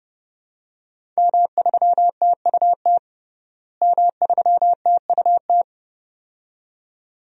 Morse code, heard as a keyed tone, about 29 wpm, 700 Hz. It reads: M3TUT M3TUT
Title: DXCC-100 Call Signs - 5 Characters - Repeated 2x - 30wpm